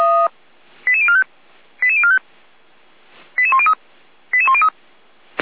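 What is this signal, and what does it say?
Five-tone selective call (ZVEI), heard as quick runs of five tones, tones of about 70 ms. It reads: 88045 88045 88103 88103
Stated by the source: ZVEI five-tone selective call 88102 received on an SDR, fire brigade (Feuerwehr) Orlamünde.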